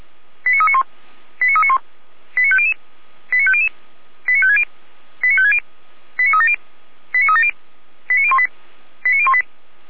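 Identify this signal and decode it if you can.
Five-tone selective call (ZVEI), heard as quick runs of five tones, tones of about 70 ms. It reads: 89391 89391 89500 89500 89560 89560 89370 89370 89018 89018
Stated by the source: ZVEI five-tone selective call received on an SDR.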